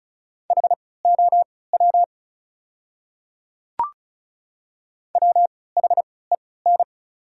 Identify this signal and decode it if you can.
Morse code, heard as a keyed tone, about 36 wpm, spaced as for 20 wpm, 690 Hz. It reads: HOW WHEN
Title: 3x Character Spacing - Top 100 Words - 35wpm